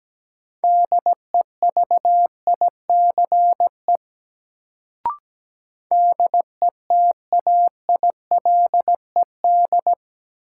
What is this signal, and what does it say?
Morse code, heard as a keyed tone, about 17 wpm, 700 Hz. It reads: DEVICE DETAILED
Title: Sentences from Top 1500 Words - New Word Drill - 17wpm